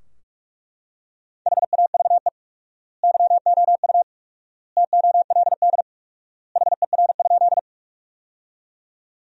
Morse code, heard as a keyed tone, about 41 wpm, 700 Hz. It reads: HAVE YOU TOLD HER?
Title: 4x Word-Spacing - Sentences from Top 400 Words - 45wpm